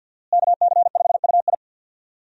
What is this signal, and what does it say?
Morse code, heard as a keyed tone, about 51 wpm, 700 Hz. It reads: KC5UI
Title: USA Call Signs - 5 Characters - Repeated 3x - 50wpm